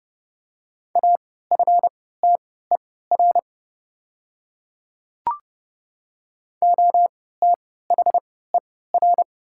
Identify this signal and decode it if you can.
Morse code, heard as a keyed tone, about 29 wpm, spaced as for 17 wpm, 700 Hz. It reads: AFTER OTHER
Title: 3x Character Spacing - Top 200 Words - 30wpm